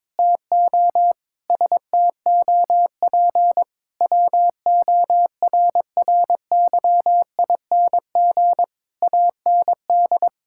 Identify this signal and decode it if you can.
Morse code, heard as a keyed tone, about 22 wpm, 700 Hz. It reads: TO STOP WORRYING AND
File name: practice / 22wpm / Sets of 8 Words - 22wpm